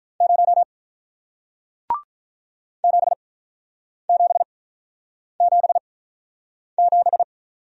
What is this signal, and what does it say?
Morse code, heard as a keyed tone, about 48 wpm, spaced as for 9 wpm, 700 Hz. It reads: 0 7777